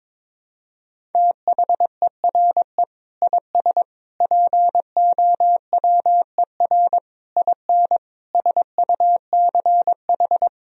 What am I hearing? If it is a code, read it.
Morse code, heard as a keyed tone, about 22 wpm, 700 Hz. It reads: THERE IS POWER IN SUCH
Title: Sentences from Top 1000 Words - Review Part 4 - 22wpm